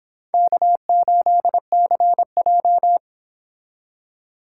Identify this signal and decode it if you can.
Morse code, heard as a keyed tone, about 25 wpm, 700 Hz. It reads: K8CJ